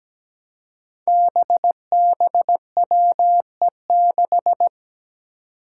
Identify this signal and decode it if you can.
Morse code, heard as a keyed tone, about 17 wpm, 700 Hz. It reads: BBWE6
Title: Five Challenging Letter-Number Combinations 17wpm